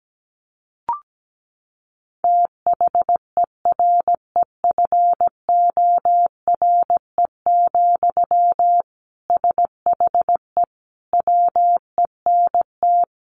Morse code, heard as a keyed tone, about 17 wpm, 700 Hz. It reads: THEREFORE, SHE WENT
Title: Sentences from Top 800 Words 17wpm